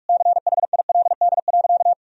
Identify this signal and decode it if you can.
Morse code, heard as a keyed tone, about 41 wpm, 700 Hz. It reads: CHILD.
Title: Sentences from Top 1000 Words - Review Part 1 - 45wpm